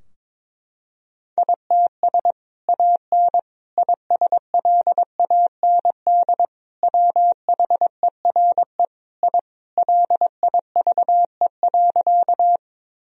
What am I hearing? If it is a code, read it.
Morse code, heard as a keyed tone, about 22 wpm, 700 Hz. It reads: ITS AN ISLAND WHERE I LIVE.
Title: Sentences from Top 1000 Words - Review Part 3 - 22wpm